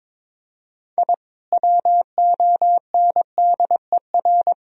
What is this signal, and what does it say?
Morse code, heard as a keyed tone, about 22 wpm, 700 Hz. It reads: I WONDER
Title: Sentences from Top 1000 Words - Review Part 3 - 22wpm